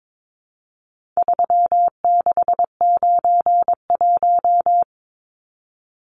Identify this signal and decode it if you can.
Morse code, heard as a keyed tone, about 22 wpm, 700 Hz. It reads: 3691
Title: Numbers - 4-Digit 22wpm